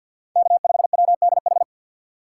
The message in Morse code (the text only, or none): K5WDH